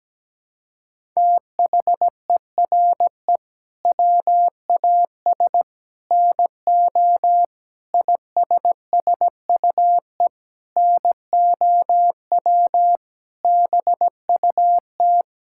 Morse code, heard as a keyed tone, about 17 wpm, 700 Hz. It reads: THERE WAS NO ISSUE NOW BUT
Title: Sentences from Top 1000 Words - Review Part 6 - 17wpm